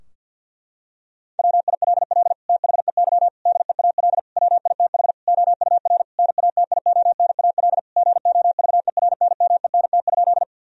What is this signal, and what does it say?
Morse code, heard as a keyed tone, about 50 wpm, 700 Hz. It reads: WILL THEY DEAL WITH OUR NATIONAL GOVERNMENT?
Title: Sentences from Top 1000 Words - Review Part 4 - 50wpm